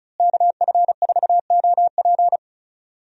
Morse code, heard as a keyed tone, about 36 wpm, 700 Hz. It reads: KF4OP
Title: USA Call Signs - 5 Characters - Repeated 3x - 35wpm